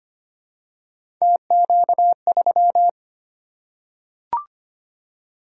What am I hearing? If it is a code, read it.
Morse code, heard as a keyed tone, about 25 wpm, 700 Hz. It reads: TQ3
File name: Three Letter-Number Combinations - 25wpm